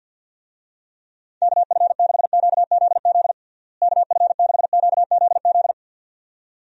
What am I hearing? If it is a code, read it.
Morse code, heard as a keyed tone, about 50 wpm, 700 Hz. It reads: KF6QZZ KF6QZZ
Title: USA Call Signs - 6 Characters - Repeated 2x - 50wpm